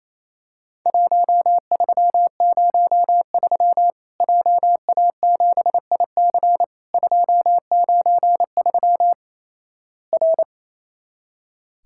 Morse code, heard as a keyed tone, about 28 wpm, 700 Hz. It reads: 1303 JA7IC 293 R
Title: Worked All Europe DX Contest - QTC Exchange - 28wpm